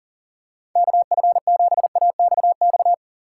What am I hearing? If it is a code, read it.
Morse code, heard as a keyed tone, about 40 wpm, 700 Hz. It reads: KF7AXX
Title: USA Call Signs - 6 Characters - 3x Character Spacing - 40wpm